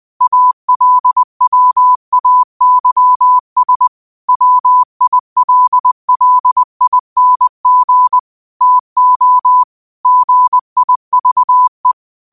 Morse code, heard as a keyed tone, about 20 wpm, 1000 Hz. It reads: ALWAYS WILLING TO GIVE
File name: qotdfiles / 20wpm